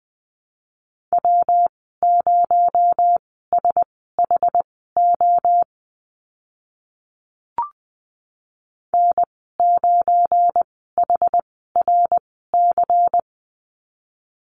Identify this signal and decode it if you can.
Morse code, heard as a keyed tone, about 20 wpm, spaced as for 14 wpm, 700 Hz. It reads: W0SHO N9HRC
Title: 219 Lesson 35g - Number 0 - Call Signs 20wpm